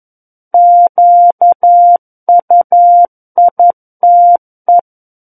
Morse code, heard as a keyed tone, about 11 wpm, 700 Hz. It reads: QUITE
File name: 11 WPM